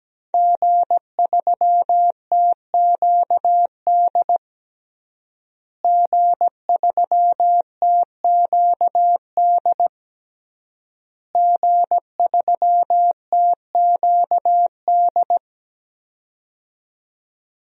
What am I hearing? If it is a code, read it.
Morse code, heard as a keyed tone, about 17 wpm, 700 Hz. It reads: G3TQD G3TQD G3TQD
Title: DXCC-100 Call Signs - 5 Characters - Repeated 3x - 17wpm